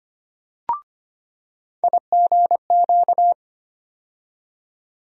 Morse code, heard as a keyed tone, about 25 wpm, 700 Hz. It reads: IGQ
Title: USA Call Sign Postfixes - 3 Letters 25wpm